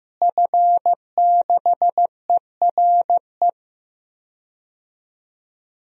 Morse code, heard as a keyed tone, about 15 wpm, 700 Hz. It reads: F6ERE
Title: DXCC-100 Call Signs - 5 Characters - Repeated 3x - 15wpm